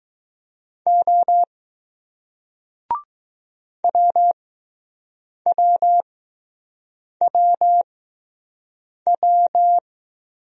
Morse code, heard as a keyed tone, about 20 wpm, spaced as for 6 wpm, 700 Hz. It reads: O WWWW